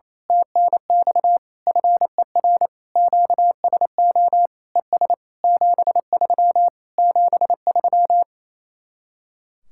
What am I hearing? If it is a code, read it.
Morse code, heard as a keyed tone, about 26 wpm, 700 Hz. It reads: TNX FER QSO ES 73 73
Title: Speed-Racing Common QSO Phrases Encore - 28wpm